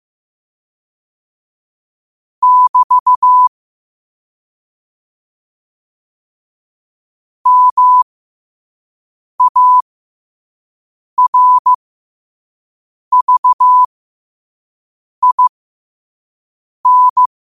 Morse code, heard as a keyed tone, about 15 wpm, spaced as for 5 wpm, 1000 Hz. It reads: <BT> MARVIN
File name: qotdfiles / 5wpm